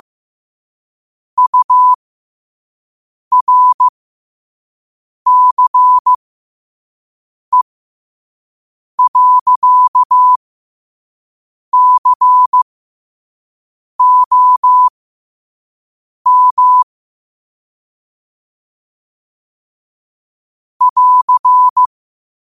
Morse code, heard as a keyed tone, about 15 wpm, spaced as for 5 wpm, 1000 Hz. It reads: URCE.COM <AR>
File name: qotdfiles / 5wpm